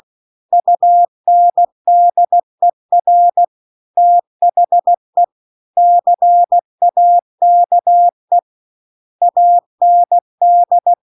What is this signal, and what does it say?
Morse code, heard as a keyed tone, about 16 wpm, 700 Hz. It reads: UNDER THE CAKE AND